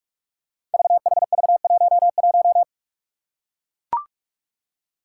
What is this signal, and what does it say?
Morse code, heard as a keyed tone, about 44 wpm, 700 Hz. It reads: VHV11
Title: Five Challenging Letter-Number Combinations 45wpm